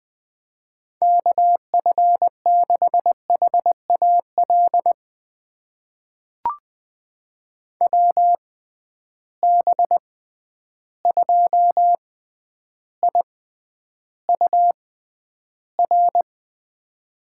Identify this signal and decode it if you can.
Morse code, heard as a keyed tone, about 21 wpm, spaced as for 10 wpm, 700 Hz. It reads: KF6HAL WB2IUR
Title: USA Call Signs - 6 Characters - 6x Character Spacing - 20wpm